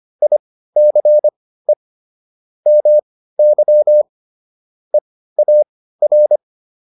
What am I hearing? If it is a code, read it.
Morse code, heard as a keyed tone, about 25 wpm, spaced as for 15 wpm, 600 Hz. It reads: ICE MY EAR